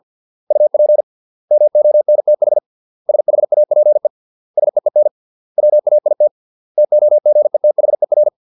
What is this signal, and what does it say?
Morse code, heard as a keyed tone, about 50 wpm, 600 Hz. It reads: UP MONTH SHAPE SEEN WAIT TOGETHER